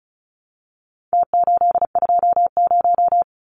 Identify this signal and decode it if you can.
Morse code, heard as a keyed tone, about 34 wpm, 700 Hz. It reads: T820
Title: Numbers - Contest Style - 4-Digit 35wpm